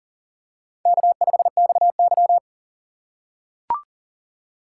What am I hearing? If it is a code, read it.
Morse code, heard as a keyed tone, about 39 wpm, 700 Hz. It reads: K5XY
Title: USA Call Signs - 4 Characters - 6x Character Spacing - 40wpm